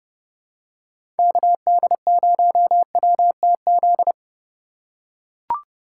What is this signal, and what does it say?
Morse code, heard as a keyed tone, about 30 wpm, 700 Hz. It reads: KD0WTZ
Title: USA Call Signs - 3x Character Spacing - 30wpm